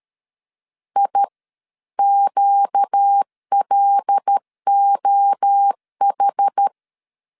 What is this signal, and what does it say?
Morse code, heard as a keyed tone, about 13 wpm, spaced as for 11 wpm, 800 Hz. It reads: IQLOH
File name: LEZIONI / 8 YJ45 / 8 Y 12wpm 2228